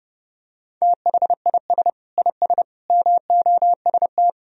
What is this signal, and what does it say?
Morse code, heard as a keyed tone, about 30 wpm, 700 Hz. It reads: THIS IS MOST
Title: Sentences from Top 1000 Words - Review Part 1 - 30wpm